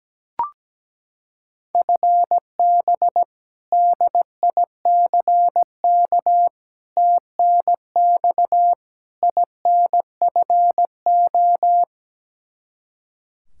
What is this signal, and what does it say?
Morse code, heard as a keyed tone, about 17 wpm, 700 Hz. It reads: FB DICK TNX INFO
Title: Practice QSO - Exchange 3 - More Info - 17wpm